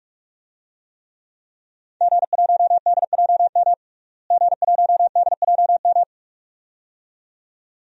Morse code, heard as a keyed tone, about 45 wpm, 700 Hz. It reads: G1DJM G1DJM
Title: DXCC-100 Call Signs - 5 Characters - Repeated 2x - 45wpm